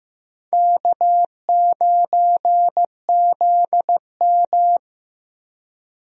Morse code, heard as a keyed tone, about 15 wpm, 700 Hz. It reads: K9ZM
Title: USA Call Signs - 4 Characters - 15wpm